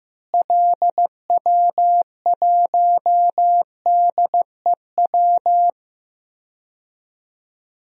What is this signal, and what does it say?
Morse code, heard as a keyed tone, about 16 wpm, 700 Hz. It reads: LW1DEW
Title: DXCC-100 Call Signs - 6 Characters - Repeated 2x - 15wpm